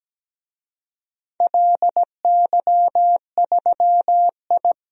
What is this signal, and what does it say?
Morse code, heard as a keyed tone, about 17 wpm, 700 Hz. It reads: LY3I
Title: DXCC-100 Call Signs - 4 Characters - 17wpm